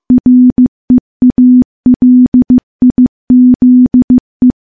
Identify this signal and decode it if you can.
Morse code, heard as a keyed tone, about 15 wpm, 250 Hz. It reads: REALIZE
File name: morse / morse-training-15WPM